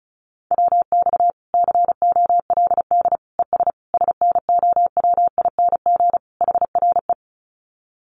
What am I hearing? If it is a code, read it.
Morse code, heard as a keyed tone, about 34 wpm, 700 Hz. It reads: WX COLD ES SNOWING HRE
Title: Common QSO Phrases 35wpm